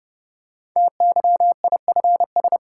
Morse code, heard as a keyed tone, about 30 wpm, 700 Hz. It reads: TYIFS
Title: Code Groups - 5 Letter:Numbers - 30wpm